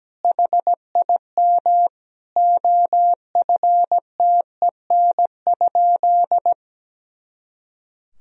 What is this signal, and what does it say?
Morse code, heard as a keyed tone, about 17 wpm, 700 Hz. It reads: HIM OFTEN?
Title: Sentences from Top 400 Words and Abbreviations 17wpm